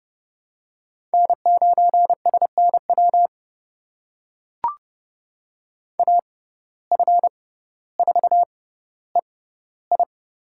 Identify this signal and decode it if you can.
Morse code, heard as a keyed tone, about 30 wpm, spaced as for 15 wpm, 700 Hz. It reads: N9SNW AF4EI
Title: USA Call Signs - 5 Characters - 6x Character Spacing - 30wpm